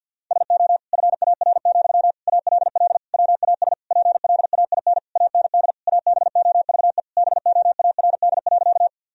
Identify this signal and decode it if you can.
Morse code, heard as a keyed tone, about 50 wpm, 700 Hz. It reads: SO FAR, ALL WAS PLAIN AND ABOVE BOARD.
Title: Sentences from Top 1000 Words - Review Part 4 - 50wpm